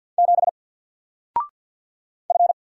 Morse code, heard as a keyed tone, about 50 wpm, 700 Hz. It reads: 7 F
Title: Single Letter-Number 50wpm